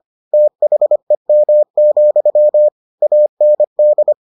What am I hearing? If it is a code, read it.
Morse code, heard as a keyed tone, about 25 wpm, 600 Hz. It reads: THEM, AND